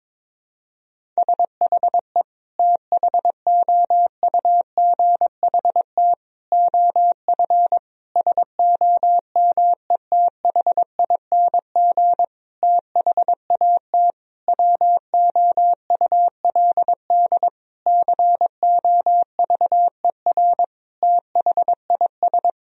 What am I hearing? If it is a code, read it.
Morse code, heard as a keyed tone, about 22 wpm, 700 Hz. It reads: SHE THOUGHT OF SOMETHING THAT WOULD COVER THIS
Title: Sentences from Top 1000 Words - Review Part 1 - 22wpm